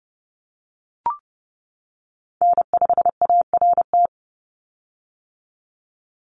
Morse code, heard as a keyed tone, about 30 wpm, 700 Hz. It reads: N5ART
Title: USA Call Signs 30wpm